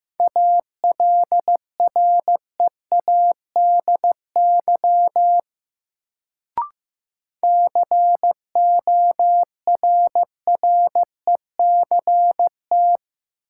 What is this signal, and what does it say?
Morse code, heard as a keyed tone, about 15 wpm, 700 Hz. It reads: ALREADY CORRECT
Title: Sentences from Top 700 Words - New Word Drill - 15wpm